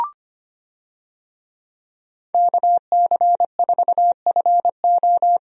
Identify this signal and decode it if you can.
Morse code, heard as a keyed tone, about 25 wpm, 690 Hz. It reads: KC4FO
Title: USA Call Signs - 5 Characters - 25wpm